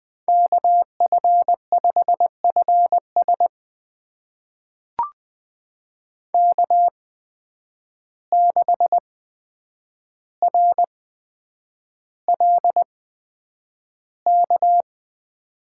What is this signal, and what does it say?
Morse code, heard as a keyed tone, about 20 wpm, spaced as for 9 wpm, 700 Hz. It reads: KF5FS K6RLK
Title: USA Call Signs - 5 Characters - 8x Character Spacing - 20wpm